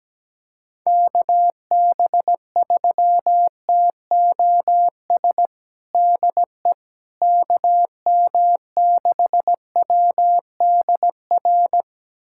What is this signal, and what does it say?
Morse code, heard as a keyed tone, about 17 wpm, 700 Hz. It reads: KB3TOS DE KM6WDR
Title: USA Callsign DE Callsign - 6 Characters - 17wpm